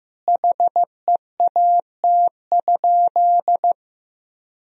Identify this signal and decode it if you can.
Morse code, heard as a keyed tone, about 16 wpm, 700 Hz. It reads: HEAT?
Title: Sentences from Top 1000 Words - Review Part 3 - 15wpm